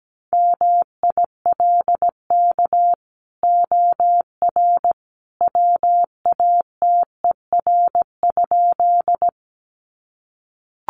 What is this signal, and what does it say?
Morse code, heard as a keyed tone, about 17 wpm, 700 Hz. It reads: MILK OR WATER?
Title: Sentences from Top 1500 Words 17wpm